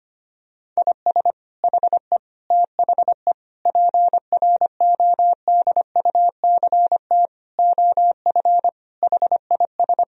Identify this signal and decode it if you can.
Morse code, heard as a keyed tone, about 25 wpm, 700 Hz. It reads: IS HE THE PRODUCT OF HIS
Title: Sentences from Top 1000 Words - Review Part 2 - 25wpm